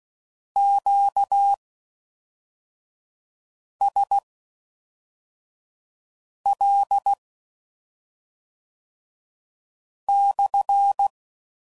Morse code, heard as a keyed tone, about 16 wpm, spaced as for 3 wpm, 800 Hz. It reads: QSL/